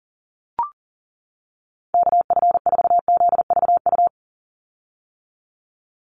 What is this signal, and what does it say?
Morse code, heard as a keyed tone, about 42 wpm, 700 Hz. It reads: KF4ZVU
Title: USA Call Signs 40wpm